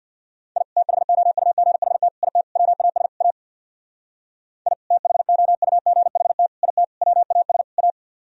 Morse code, heard as a keyed tone, about 64 wpm, 700 Hz. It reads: I THOUGHT IT WAS A I THOUGHT IT WAS A